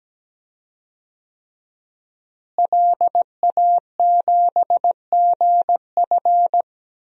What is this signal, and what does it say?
Morse code, heard as a keyed tone, about 17 wpm, 700 Hz. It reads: LA7GF
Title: DXCC-100 Call Signs - 5 Characters - Repeated 3x - 17wpm